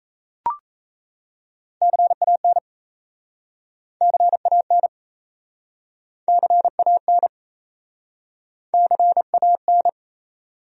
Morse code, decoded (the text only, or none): CAN CAN CAN CAN